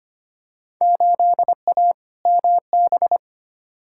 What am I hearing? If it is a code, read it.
Morse code, heard as a keyed tone, about 25 wpm, 700 Hz. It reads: 8A MB
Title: Field Day Exchanges Encore - 25wpm